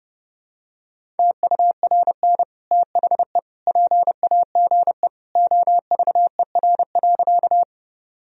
Morse code, heard as a keyed tone, about 30 wpm, 700 Hz. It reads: TURN THE PAGE OVER.
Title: Sentences from Top 1000 Words - Review Part 1 - 30wpm